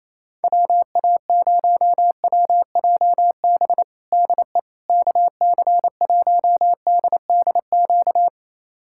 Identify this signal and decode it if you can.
Morse code, heard as a keyed tone, about 28 wpm, 700 Hz. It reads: WA0WJB DE KC1DDQ
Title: USA Callsign DE Callsign - 6 Characters - 28wpm